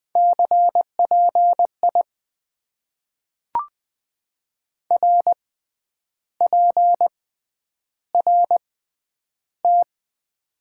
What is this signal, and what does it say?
Morse code, heard as a keyed tone, about 20 wpm, spaced as for 9 wpm, 700 Hz. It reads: CPI RPRT
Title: Top 90 QSO Elements - 6x Character Spacing - 20wpm